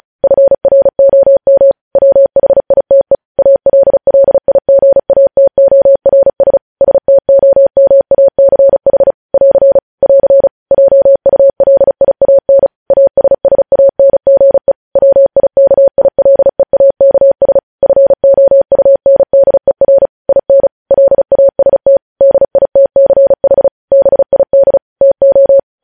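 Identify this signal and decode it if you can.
Morse code, heard as a keyed tone, about 36 wpm, 550 Hz. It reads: FROM WHITE ALLIGATORS STOMACH <AR> <AR> JULIAN ASSANGE WIKILEAKS FOUNDER IN LAST DITCH BID TO
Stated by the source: Morse code news headlines sent at 35 wpm.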